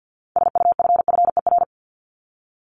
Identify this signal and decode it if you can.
Morse code, heard as a keyed tone, about 50 wpm, 700 Hz. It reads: SUFFER
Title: ING Suffix - Top 100 Words with Base Words 50wpm